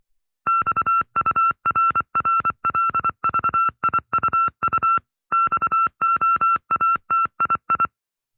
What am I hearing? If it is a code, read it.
Morse code, heard as a keyed tone, about 25 wpm, 1350 Hz. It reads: XURRLVIUU XOATII
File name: FreeFiles / CW-practice / 20WPM-side1